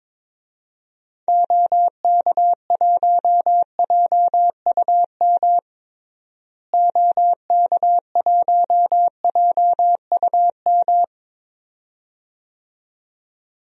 Morse code, heard as a keyed tone, about 22 wpm, 700 Hz. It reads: OK1JUM OK1JUM
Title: DXCC-100 Call Signs - 6 Characters - Repeated 2x - 22wpm